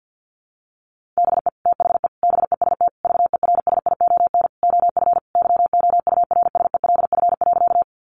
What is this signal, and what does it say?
Morse code, heard as a keyed tone, about 48 wpm, 700 Hz. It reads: BE THE BEST VERSION OF YOURSELF.
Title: Sentences from Top 1400 Words 50wpm